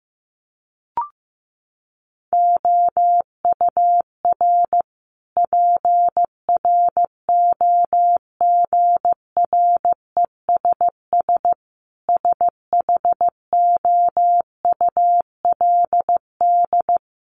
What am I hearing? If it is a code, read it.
Morse code, heard as a keyed tone, about 15 wpm, 700 Hz. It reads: OUR PROGRESS SHOULD
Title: Sentences from Top 1400 Words 15wpm